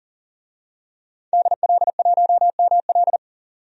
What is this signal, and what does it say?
Morse code, heard as a keyed tone, about 40 wpm, 700 Hz. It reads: DL1ML
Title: DXCC-100 Call Signs - 5 Characters - 40wpm